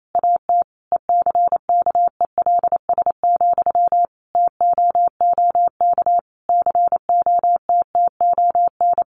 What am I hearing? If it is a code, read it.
Morse code, heard as a keyed tone, about 28 wpm, 700 Hz. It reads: AT ECKELS, TOOK COTTON
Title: A Sound Of Thunder 28wpm